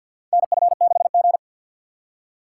Morse code, heard as a keyed tone, about 49 wpm, 700 Hz. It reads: NF6G
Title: USA Call Signs - 3x Character Spacing - 50wpm